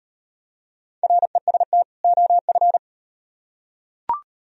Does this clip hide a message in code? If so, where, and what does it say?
Yes: Morse code from 1.0 to 2.8 s, heard as a keyed tone, about 38 wpm, 700 Hz. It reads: REST OF